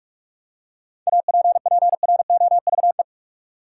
Morse code, heard as a keyed tone, about 46 wpm, 700 Hz. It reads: APPROVE
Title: Seven Letter Words - 45wpm